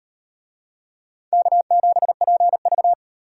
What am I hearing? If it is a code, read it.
Morse code, heard as a keyed tone, about 36 wpm, 700 Hz. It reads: K7PV